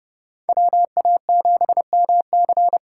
Morse code, heard as a keyed tone, about 30 wpm, 700 Hz. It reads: WA7MC